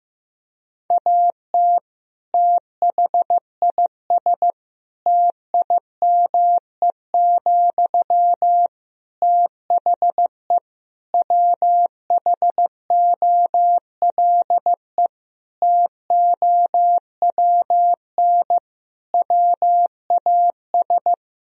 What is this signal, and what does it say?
Morse code, heard as a keyed tone, about 15 wpm, 700 Hz. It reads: AT THIS TIME, THE WHOLE TOWN WAS
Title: Sentences from Top 1000 Words - Review Part 2 - 15wpm